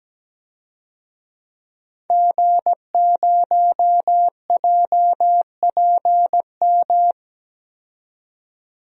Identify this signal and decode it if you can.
Morse code, heard as a keyed tone, about 17 wpm, 700 Hz. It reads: G0JPM
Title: DXCC-100 Call Signs - 5 Characters - 17wpm